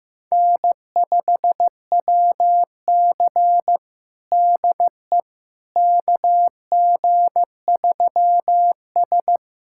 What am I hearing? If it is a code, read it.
Morse code, heard as a keyed tone, about 16 wpm, 700 Hz. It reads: N5WC DE KG3S
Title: USA Callsign DE Callsign - 4 Characters - 15wpm